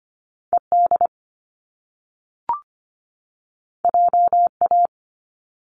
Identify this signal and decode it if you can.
Morse code, heard as a keyed tone, about 25 wpm, 700 Hz. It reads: ED JA
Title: Top Two Letter Combinations in Words 25wpm